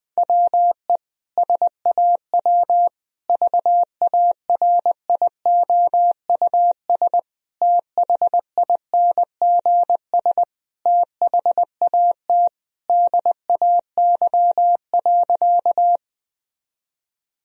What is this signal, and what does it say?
Morse code, heard as a keyed tone, about 21 wpm, 700 Hz. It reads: WE SAW VARIOUS THINGS THAT DAY.
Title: Sentences from Top 1000 Words - Review Part 4 - 20wpm